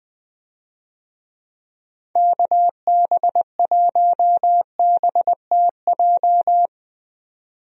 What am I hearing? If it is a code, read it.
Morse code, heard as a keyed tone, about 20 wpm, 700 Hz. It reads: KB1BTJ
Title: USA Call Signs - 6 Characters - 20wpm